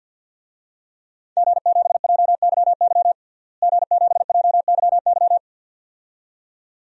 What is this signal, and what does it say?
Morse code, heard as a keyed tone, about 52 wpm, 700 Hz. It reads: G7JYY G7JYY
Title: DXCC-100 Call Signs - 5 Characters - Repeated 2x - 50wpm